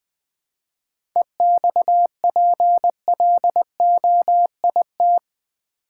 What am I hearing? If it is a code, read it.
Morse code, heard as a keyed tone, about 20 wpm, 700 Hz. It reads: EXPLOIT
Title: 7 Letter Words - First 3 Letters Given - 20wpm